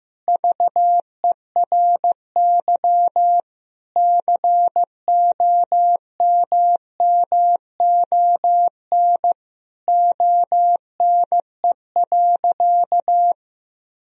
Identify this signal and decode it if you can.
Morse code, heard as a keyed tone, about 15 wpm, 700 Hz. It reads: VERY COMMON ONE.
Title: Sentences from Top 1000 Words - Review Part 4 - 15wpm